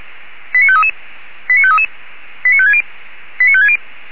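Five-tone selective call (ZVEI), heard as quick runs of five tones, tones of about 70 ms. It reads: 89533 89533 89560 89560
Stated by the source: Five-tone ZVEI selective call received by SDR.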